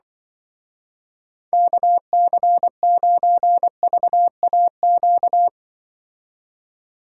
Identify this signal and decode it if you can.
Morse code, heard as a keyed tone, about 25 wpm, 700 Hz. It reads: KC9VAQ
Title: Speed Racing - USA 2x3 Call Signs 20wpm